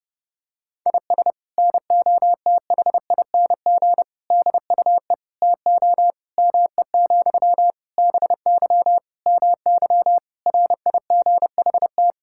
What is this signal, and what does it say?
Morse code, heard as a keyed tone, about 30 wpm, 700 Hz. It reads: IS NOTHING DUE TO ME, BY MY RIGHT
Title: Sentences from Top 1000 Words - Review Part 6 - 30wpm